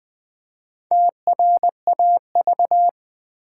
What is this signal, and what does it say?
Morse code, heard as a keyed tone, about 20 wpm, 690 Hz. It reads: TRAV
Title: Build A Word - 2 Letters At A Time - 6-Letter Words - 20wpm